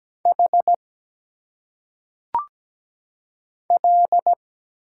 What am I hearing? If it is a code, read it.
Morse code, heard as a keyed tone, about 17 wpm, 700 Hz. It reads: H L
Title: Single Letters - Phonetic Pronunciation 17wpm